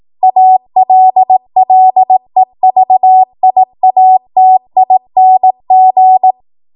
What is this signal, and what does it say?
Morse code, heard as a keyed tone, about 18 wpm, 750 Hz. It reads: ALLEVIATING
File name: Morse / 18WPM